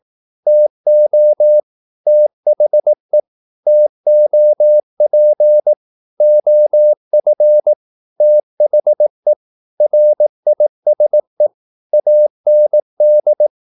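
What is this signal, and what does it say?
Morse code, heard as a keyed tone, about 18 wpm, 600 Hz. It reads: TO THE TOP OF THE RISE AND
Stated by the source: Morse practice at 18 wpm, English text, from CW med Gnister.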